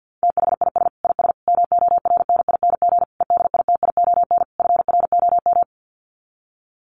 Morse code, heard as a keyed tone, about 48 wpm, 700 Hz. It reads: THIS IS MORNING EDITION FROM
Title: Sets of 5 Words Encore2 50wpm